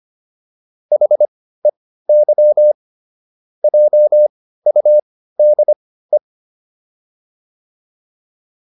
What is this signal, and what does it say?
Morse code, heard as a keyed tone, about 25 wpm, spaced as for 15 wpm, 600 Hz. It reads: HEY JUDE